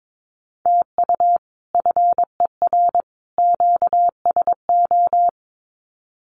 Morse code, heard as a keyed tone, about 22 wpm, 700 Hz. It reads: TU FER QSO
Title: Common QSO Phrases 22wpm